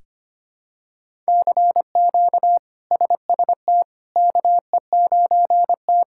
Morse code, heard as a keyed tone, about 25 wpm, 700 Hz. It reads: CQ SST KE9T
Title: K1USN SST QSO Practice - 25wpm